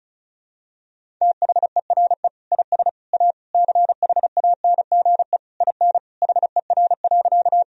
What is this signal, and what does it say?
Morse code, heard as a keyed tone, about 34 wpm, 700 Hz. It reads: THERE IS A CHANGE IN HER.
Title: Sentences from Top 1000 Words - Review Part 1 - 35wpm